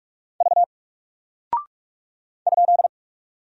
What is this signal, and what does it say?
Morse code, heard as a keyed tone, about 42 wpm, 700 Hz. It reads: V ?
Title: Single Character (Letter:Number:Punctuation:Prosign) Alt 45wpm